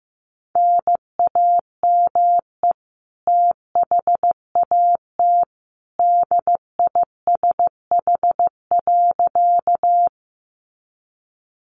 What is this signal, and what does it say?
Morse code, heard as a keyed tone, about 15 wpm, 700 Hz. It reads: NAME THAT DISH.
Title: Sentences from Top 1300 Words 15wpm